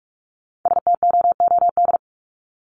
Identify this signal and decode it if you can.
Morse code, heard as a keyed tone, about 44 wpm, 700 Hz. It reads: STOOD
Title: Top 500 Words 45wpm